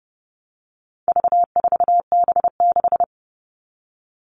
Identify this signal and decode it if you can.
Morse code, heard as a keyed tone, about 30 wpm, 700 Hz. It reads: V4B6
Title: All Cut Numbers - 4-Digit 30wpm